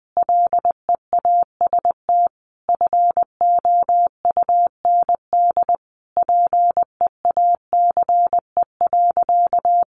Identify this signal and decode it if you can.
Morse code, heard as a keyed tone, about 20 wpm, 700 Hz. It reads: LEAST FOUND PEACE.
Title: Sentences from Top 1500 Words 20wpm